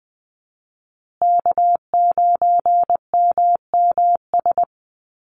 Morse code, heard as a keyed tone, about 20 wpm, 700 Hz. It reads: K9MMS
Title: Call Signs Repeated 2x - 20wpm